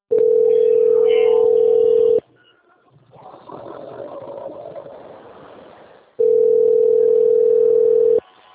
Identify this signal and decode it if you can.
Ringback tone (North America), 440 + 480 Hz.